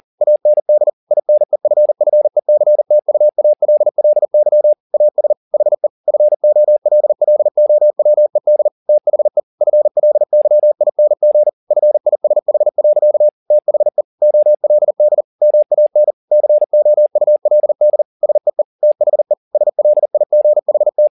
Morse code, heard as a keyed tone, about 40 wpm, 600 Hz. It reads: AND INEFFECTUALLY AS HE FOLLOWED THE FLYING FISH. THE OLD MAN COULD SEE THE SLIGHT